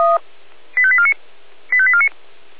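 Five-tone selective call (ZVEI), heard as quick runs of five tones, tones of about 70 ms. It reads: 85849 85849
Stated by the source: ZVEI five-tone selective call 85874, received by SDR.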